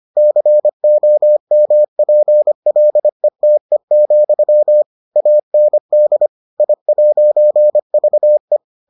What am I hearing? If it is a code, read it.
Morse code, heard as a keyed tone, about 25 wpm, 600 Hz. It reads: COMPLETE, AND I'VE